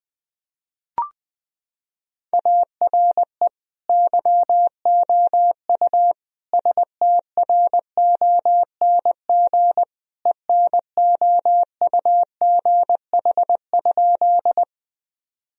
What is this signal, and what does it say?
Morse code, heard as a keyed tone, about 20 wpm, 700 Hz. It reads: ARE YOU STRONG ENOUGH?
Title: Sentences from Top 1000 Words - Review Part 3 - 20wpm